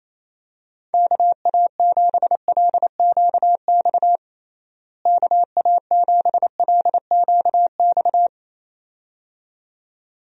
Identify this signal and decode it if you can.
Morse code, heard as a keyed tone, about 28 wpm, 700 Hz. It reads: KA7LQX KA7LQX
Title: USA Call Signs - 6 Characters - Repeated 2x - 28wpm